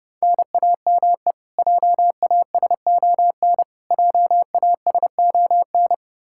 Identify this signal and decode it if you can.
Morse code, heard as a keyed tone, about 30 wpm, 700 Hz. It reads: NAME JASON JASON
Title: Practice QSO - Exchange 1 - Beginning - 30wpm